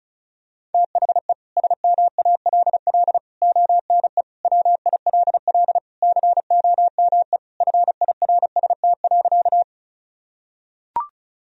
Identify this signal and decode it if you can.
Morse code, heard as a keyed tone, about 36 wpm, 700 Hz. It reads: THE SMALL ONE WILL COME FIRST.
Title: Sentences from Top 1000 Words - Review Part 1 - 35wpm